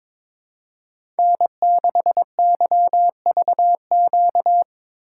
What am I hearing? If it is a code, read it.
Morse code, heard as a keyed tone, about 22 wpm, 700 Hz. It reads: N6YVQ
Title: USA Call Signs - 5 Characters - 4x Character Spacing - 22wpm